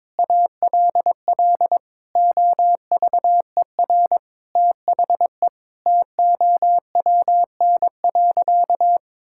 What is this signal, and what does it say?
Morse code, heard as a keyed tone, about 22 wpm, 700 Hz. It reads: ALL OVER THE TOWN.
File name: practice / 22wpm / Sentences with 3-Word Phrases - 22wpm